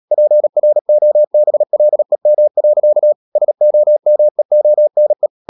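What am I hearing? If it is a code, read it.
Morse code, heard as a keyed tone, about 37 wpm, 600 Hz. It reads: PROBLEM. SOMEONE
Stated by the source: Morse practice at 36 wpm, text of Ken Follett's novel Never, from CW med Gnister.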